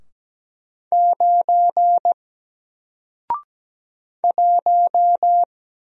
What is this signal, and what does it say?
Morse code, heard as a keyed tone, about 17 wpm, 700 Hz. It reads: 9 1
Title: Single Letter-Number Alt 17wpm